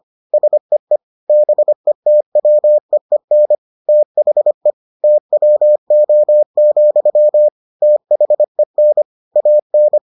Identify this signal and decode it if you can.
Morse code, heard as a keyed tone, about 25 wpm, 600 Hz. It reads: SEE BETWEEN THE TWO, THEN AN